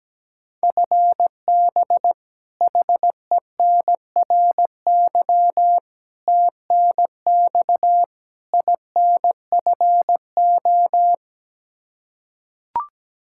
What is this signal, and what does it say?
Morse code, heard as a keyed tone, about 17 wpm, 700 Hz. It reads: FB HENRY TNX INFO
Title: Practice QSO - Exchange 3 - More Info - 17wpm